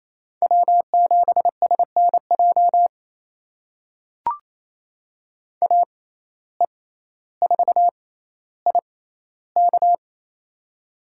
Morse code, heard as a keyed tone, about 28 wpm, spaced as for 14 wpm, 700 Hz. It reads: W7SNJ AE4IK
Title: USA Call Signs - 5 Characters - 6x Character Spacing - 28wpm